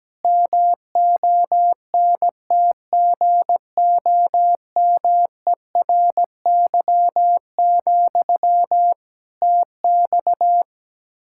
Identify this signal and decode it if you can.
Morse code, heard as a keyed tone, about 17 wpm, 700 Hz. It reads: MONTGOMERY, TX
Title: K1USN SST QSO Practice 17wpm